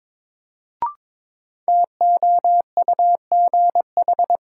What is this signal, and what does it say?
Morse code, heard as a keyed tone, about 22 wpm, 700 Hz. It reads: TOUGH
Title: Sentences from Top 1200 Words - New Word Drill - 22wpm